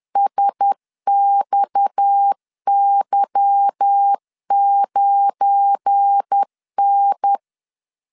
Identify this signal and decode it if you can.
Morse code, heard as a keyed tone, about 10 wpm, 800 Hz. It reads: SXY9N
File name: LEZIONI / 8 YJ45 / 8 Y 10wpm 2615